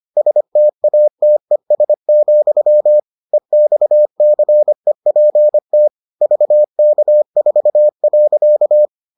Morse code, heard as a keyed tone, about 25 wpm, 600 Hz. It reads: STATES, EXCEPT VK4.